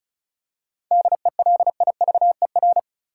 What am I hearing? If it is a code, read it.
Morse code, heard as a keyed tone, about 34 wpm, 700 Hz. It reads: DELIVER
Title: Seven Letter Words - 35wpm